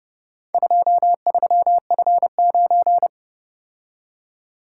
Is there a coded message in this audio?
Yes: Morse code from 0.5 to 3.1 s, heard as a keyed tone, about 30 wpm, 700 Hz. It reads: 23F9